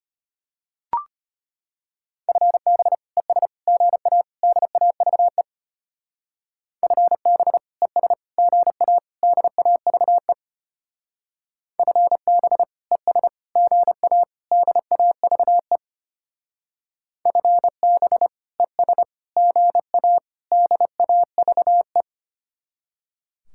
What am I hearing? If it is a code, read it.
Morse code, heard as a keyed tone, about 32 wpm, 700 Hz. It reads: FB ES GA DAVE FB ES GA DAVE FB ES GA DAVE FB ES GA DAVE